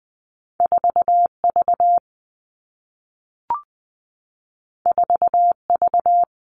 Morse code, heard as a keyed tone, about 20 wpm, 700 Hz. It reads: 4V 4V
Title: All Cut Numbers - 2-Digit 20wpm